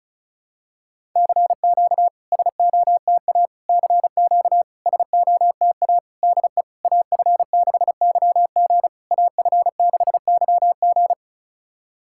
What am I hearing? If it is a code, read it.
Morse code, heard as a keyed tone, about 36 wpm, 700 Hz. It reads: CQ SOTA CQ SOTA DE AF6YG AF6YG
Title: SOTA QSO Practice 35wpm